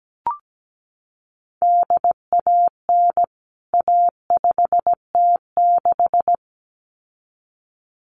Morse code, heard as a keyed tone, about 17 wpm, 700 Hz. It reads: DAN A5T6
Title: CWT Exchange 17wpm